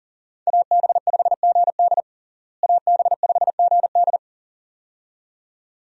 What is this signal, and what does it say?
Morse code, heard as a keyed tone, about 40 wpm, 700 Hz. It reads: AB5GD AB5GD
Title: USA Call Signs - 5 Characters - Repeated 2x - 40wpm